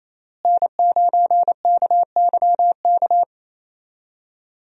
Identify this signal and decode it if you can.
Morse code, heard as a keyed tone, about 28 wpm, 700 Hz. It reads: N9KYK
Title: USA Call Signs - 5 Characters - Repeated 2x - 28wpm